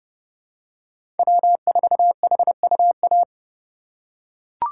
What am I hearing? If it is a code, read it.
Morse code, heard as a keyed tone, about 30 wpm, 700 Hz. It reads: W4HUA
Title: USA Call Signs - 5 Characters - 4x Character Spacing - 30wpm